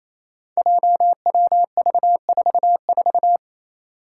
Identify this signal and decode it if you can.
Morse code, heard as a keyed tone, about 28 wpm, 700 Hz. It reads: JWV44
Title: Five Challenging Letter-Number Combinations 28wpm